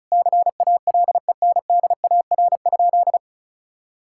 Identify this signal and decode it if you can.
Morse code, heard as a keyed tone, about 36 wpm, 700 Hz. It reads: CALENDAR?
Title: Sentences from Top 1600 Words 35wpm